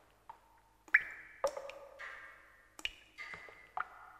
Emergency Alert System (SAME) tones, heard as harsh buzzing bursts. It is heard from 0.9 to 3.7 s.